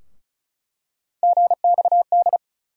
Morse code, heard as a keyed tone, about 36 wpm, 700 Hz. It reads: GXD
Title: USA Call Sign Postfixes - 3 Letters 35wpm